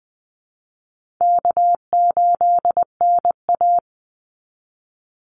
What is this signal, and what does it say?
Morse code, heard as a keyed tone, about 20 wpm, 700 Hz. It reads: K8NA
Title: Easy Call Signs Repeated 2x - 20wpm